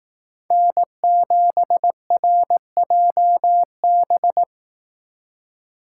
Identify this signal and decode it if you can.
Morse code, heard as a keyed tone, about 18 wpm, 700 Hz. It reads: N7RJB
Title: Speed Racing - USA 1x3 Call Signs 15wpm